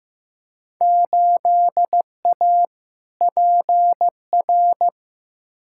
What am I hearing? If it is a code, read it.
Morse code, heard as a keyed tone, about 15 wpm, 700 Hz. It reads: 8A PR
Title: Field Day Exchanges Encore - 15wpm